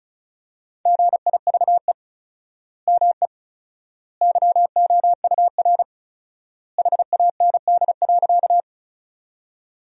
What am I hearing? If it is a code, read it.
Morse code, heard as a keyed tone, about 36 wpm, 700 Hz. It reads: GIVE ME YOUR HAND.
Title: 4x Word-Spacing - Sentences from Top 200 Words - 35wpm